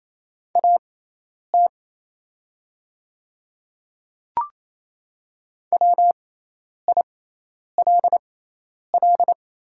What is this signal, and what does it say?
Morse code, heard as a keyed tone, about 28 wpm, spaced as for 10 wpm, 700 Hz. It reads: AT WILL
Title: 6x Character Spacing - Top 200 Words - 28wpm